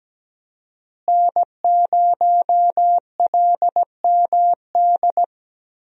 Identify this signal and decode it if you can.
Morse code, heard as a keyed tone, about 17 wpm, 700 Hz. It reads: N0LMD
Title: USA Call Signs - 5 Characters - Repeated 2x - 17wpm